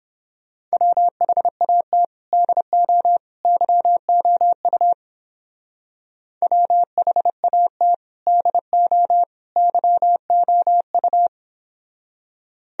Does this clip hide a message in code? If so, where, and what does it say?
Yes: Morse code from 0.7 to 11.3 s, heard as a keyed tone, about 29 wpm, 700 Hz. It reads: WHAT DO YOU WHAT DO YOU